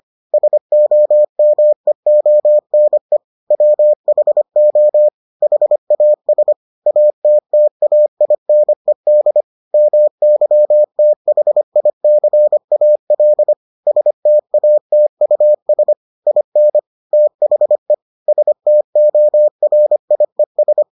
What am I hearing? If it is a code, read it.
Morse code, heard as a keyed tone, about 25 wpm, 600 Hz. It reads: SOMEONE WHO HAS ATTAINED MYTHICAL STATUS IN THE STORIES